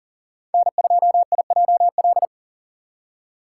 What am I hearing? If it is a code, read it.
Morse code, heard as a keyed tone, about 40 wpm, 700 Hz. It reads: N2IJL
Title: USA Call Signs - 5 Characters - 2x Character Spacing - 40wpm